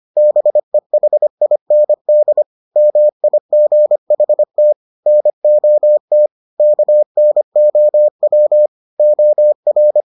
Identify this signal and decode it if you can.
Morse code, heard as a keyed tone, about 25 wpm, 600 Hz. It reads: BEHIND MIGHT NOT KNOW OR